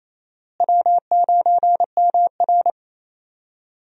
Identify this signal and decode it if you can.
Morse code, heard as a keyed tone, about 28 wpm, 700 Hz. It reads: W9MR
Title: USA Call Signs - 4 Characters - Repeated 3x - 28wpm